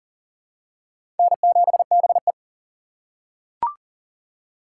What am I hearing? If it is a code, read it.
Morse code, heard as a keyed tone, about 40 wpm, 700 Hz. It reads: N7BE
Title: USA Call Signs - 3x Character Spacing - 40wpm